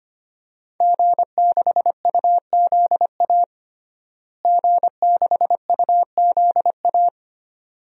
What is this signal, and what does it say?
Morse code, heard as a keyed tone, about 25 wpm, 700 Hz. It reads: G6UZA G6UZA
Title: DXCC-100 Call Signs - 5 Characters - Repeated 2x - 25wpm